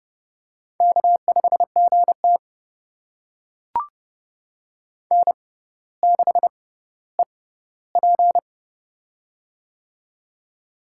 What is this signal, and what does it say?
Morse code, heard as a keyed tone, about 31 wpm, spaced as for 16 wpm, 700 Hz. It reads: K5GT N6EP
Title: USA Call Signs - 4 Characters - 6x Character Spacing - 30wpm